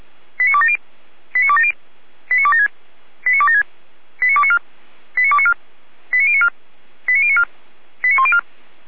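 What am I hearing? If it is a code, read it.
Five-tone selective call (ZVEI), heard as quick runs of five tones, tones of about 70 ms. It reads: 89270 89270 89276 89276 89294 89294 89094 89094 89104
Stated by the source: ZVEI five-tone selective call 89262 received on SDR by fire brigade Orlamünde (feuerwehr-orlamuende).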